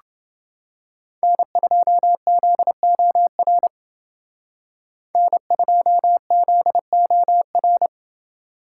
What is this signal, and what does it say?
Morse code, heard as a keyed tone, about 28 wpm, 700 Hz. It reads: N2ZOR N2ZOR